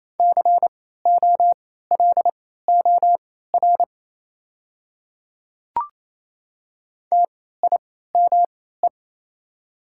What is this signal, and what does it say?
Morse code, heard as a keyed tone, about 28 wpm, spaced as for 16 wpm, 690 Hz. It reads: COLOR TIME
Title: 3x Character Spacing - Top 500 Words - 28wpm